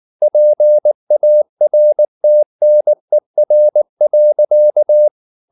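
Morse code, heard as a keyed tone, about 19 wpm, 600 Hz. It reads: PARTNER.
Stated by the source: Morse code at 19 wpm from CW med Gnister, reading Ken Follett's novel Never in English.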